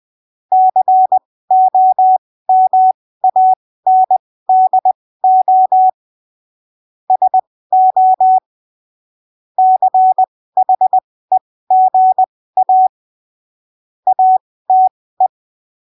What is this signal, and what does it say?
Morse code, heard as a keyed tone, about 20 wpm, spaced as for 15 wpm, 750 Hz. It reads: COMANDO SO CHEGA ATE
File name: arquivos / texto / texto 13 20ppm